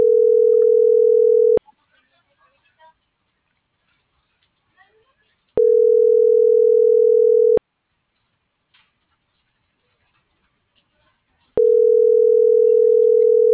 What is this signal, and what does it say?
Ringback tone (North America), 440 + 480 Hz.
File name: CENTRAL SAGOT / out omar.esagot